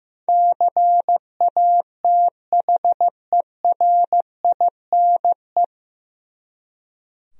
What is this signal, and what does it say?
Morse code, heard as a keyed tone, about 15 wpm, 700 Hz. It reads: CATHERINE
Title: USA Common Names - 15wpm